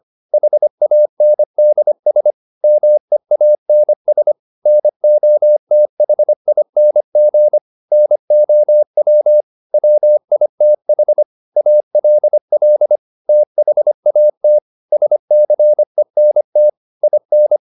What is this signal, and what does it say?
Morse code, heard as a keyed tone, about 25 wpm, 600 Hz. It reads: HANDS MEANS NOTHING NOW WITH ALL THAT SCENT IN